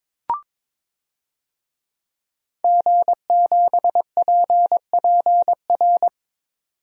Morse code, heard as a keyed tone, about 22 wpm, 700 Hz. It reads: G7PPR
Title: DXCC-100 Call Signs - 5 Characters - 22wpm